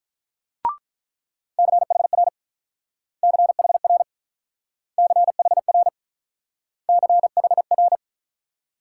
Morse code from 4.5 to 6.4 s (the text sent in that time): CHR